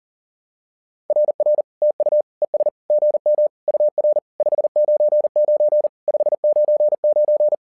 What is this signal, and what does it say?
Morse code, heard as a keyed tone, about 40 wpm, 600 Hz. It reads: RR TU ES GM UR 599 599
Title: SOTA QSO Practice 40wpm